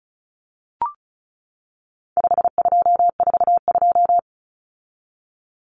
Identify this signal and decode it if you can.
Morse code, heard as a keyed tone, about 34 wpm, 700 Hz. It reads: 5242
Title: Numbers - 4-Digit 35wpm